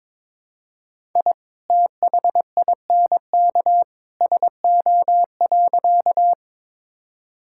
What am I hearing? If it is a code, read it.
Morse code, heard as a keyed tone, about 22 wpm, 700 Hz. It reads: I THINK SO.
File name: practice / 22wpm / Sentences with 2-Word Phrases - 22wpm